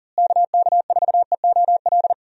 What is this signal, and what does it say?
Morse code, heard as a keyed tone, about 40 wpm, 700 Hz. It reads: KK4EOL
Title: USA Call Signs - 6 Characters - Repeated 3x - 40wpm